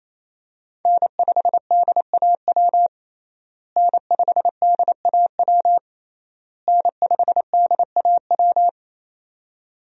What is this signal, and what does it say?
Morse code, heard as a keyed tone, about 28 wpm, 700 Hz. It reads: N5DAW N5DAW N5DAW
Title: USA Call Signs - 5 Characters - Repeated 3x - 28wpm